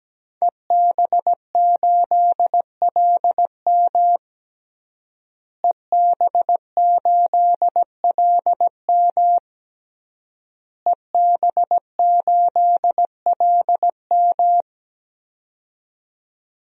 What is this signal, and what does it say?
Morse code, heard as a keyed tone, about 17 wpm, 700 Hz. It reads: EB8LM EB8LM EB8LM
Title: DXCC-100 Call Signs - 5 Characters - Repeated 3x - 17wpm